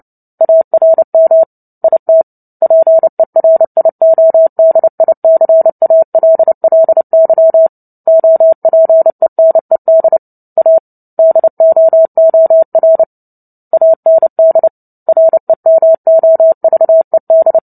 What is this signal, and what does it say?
Morse code, heard as a keyed tone, about 30 wpm, 650 Hz. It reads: ARM IT PERIODICALLY OPENED A DOOR AND REMOVED
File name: Book Two / 03-Chapter 3 - 29 WPM